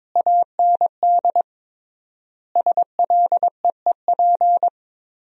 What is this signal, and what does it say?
Morse code, heard as a keyed tone, about 22 wpm, 700 Hz. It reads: AND SLEEP
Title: 3x Word-Spacing - Sentences from Top 500 Words - 22wpm